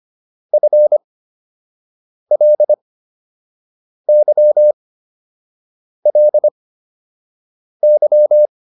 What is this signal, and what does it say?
Morse code, heard as a keyed tone, about 25 wpm, spaced as for 6 wpm, 600 Hz. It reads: FLYLY